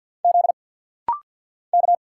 Morse code, heard as a keyed tone, about 50 wpm, 700 Hz. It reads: Z K
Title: Single Letters - Rapid-Fire 50wpm